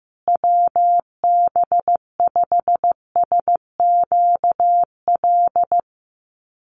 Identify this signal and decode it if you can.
Morse code, heard as a keyed tone, about 15 wpm, 700 Hz. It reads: WB5SQL
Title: USA Call Signs Repeated 3x 15wpm